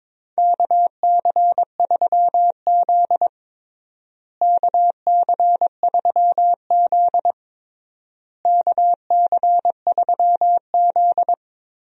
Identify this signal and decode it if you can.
Morse code, heard as a keyed tone, about 22 wpm, 700 Hz. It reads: KC3Z KC3Z KC3Z
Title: USA Call Signs - 4 Characters - Repeated 3x - 22wpm